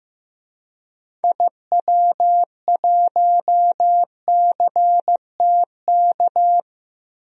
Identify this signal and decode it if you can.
Morse code, heard as a keyed tone, about 15 wpm, 700 Hz. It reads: IW1CTK
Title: DXCC-100 Call Signs - 6 Characters - 15wpm